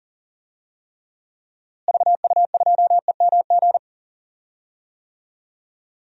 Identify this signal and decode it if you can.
Morse code, heard as a keyed tone, about 40 wpm, 700 Hz. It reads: VU2EMG
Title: DXCC-100 Call Signs - 6 Characters - Repeated 2x - 40wpm